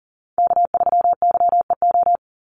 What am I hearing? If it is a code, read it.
Morse code, heard as a keyed tone, about 40 wpm, 700 Hz. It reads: K3YEO
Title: Call Signs Repeated 2x - 40wpm